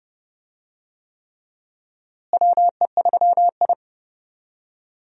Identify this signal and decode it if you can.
Morse code, heard as a keyed tone, about 30 wpm, 700 Hz. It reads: WE3I
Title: USA Call Signs - 4 Characters - Repeated 3x - 30wpm